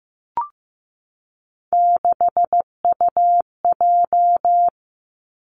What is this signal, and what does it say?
Morse code, heard as a keyed tone, about 15 wpm, 700 Hz. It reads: 6UJ
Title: Three Challenging Letter-Number Combinations 15wpm